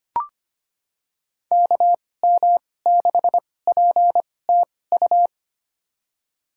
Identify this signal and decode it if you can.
Morse code, heard as a keyed tone, about 25 wpm, spaced as for 18 wpm, 700 Hz. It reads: KM6PTU
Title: USA Call Signs - 6 Characters - 2x Character Spacing - 25wpm